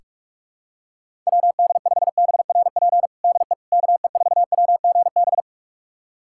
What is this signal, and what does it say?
Morse code, heard as a keyed tone, about 46 wpm, 700 Hz. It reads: WD5BRP DE KE4WGB
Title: USA Callsign DE Callsign 45wpm